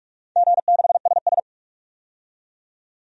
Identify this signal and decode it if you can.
Morse code, heard as a keyed tone, about 47 wpm, 700 Hz. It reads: G6SS
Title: DXCC-100 Call Signs - 4 Characters - 45wpm